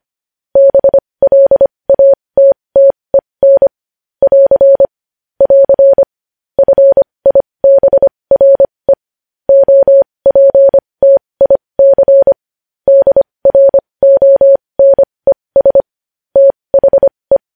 Morse code, heard as a keyed tone, about 25 wpm, spaced as for 20 wpm, 550 Hz. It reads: BLATTEN <AR> <AR> FIBRE OPTIC DRONES THE
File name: podcast / News headlines-1x-25x20